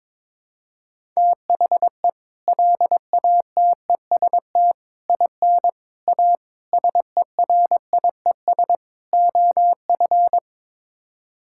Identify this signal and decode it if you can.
Morse code, heard as a keyed tone, about 22 wpm, 700 Hz. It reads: THE LATEST IN A SERIES OF
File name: practice / 22wpm / Sets of 6 Words - 22wpm